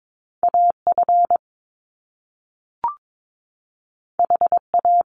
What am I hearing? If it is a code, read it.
Morse code, heard as a keyed tone, about 22 wpm, 700 Hz. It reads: AF HA
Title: Top Two Letter Combinations in Words 22wpm